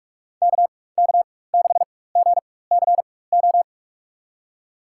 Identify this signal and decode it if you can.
Morse code, heard as a keyed tone, about 46 wpm, spaced as for 21 wpm, 700 Hz. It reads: KK6GCO
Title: USA Call Signs - 4x Character Spacing - 45wpm